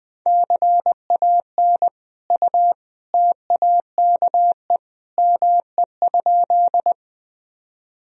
Morse code, heard as a keyed tone, about 21 wpm, 700 Hz. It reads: CAN U TAKE ME?
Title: Sentences from Top 200 Words and Abbreviations 20wpm